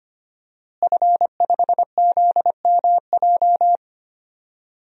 Morse code, heard as a keyed tone, about 25 wpm, 700 Hz. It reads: F5ZMJ
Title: DXCC-100 Call Signs - 5 Characters - Repeated 2x - 25wpm